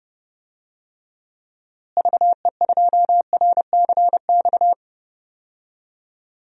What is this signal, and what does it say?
Morse code, heard as a keyed tone, about 32 wpm, 700 Hz. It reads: VE2RCX
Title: DXCC-100 Call Signs - 6 Characters - Repeated 3x - 30wpm